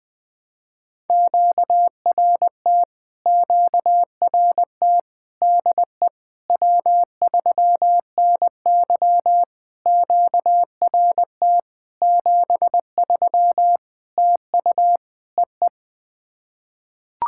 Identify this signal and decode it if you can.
Morse code, heard as a keyed tone, about 20 wpm, 690 Hz. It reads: QRT QRT DE W3NY QRT 73 TU EE
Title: POTA QSO Practice - 20wpm